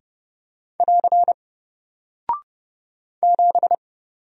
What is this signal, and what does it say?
Morse code, heard as a keyed tone, about 30 wpm, 700 Hz. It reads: <AR> 7
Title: Single Character (Letter:Number:Punctuation:Prosign) Alt 30wpm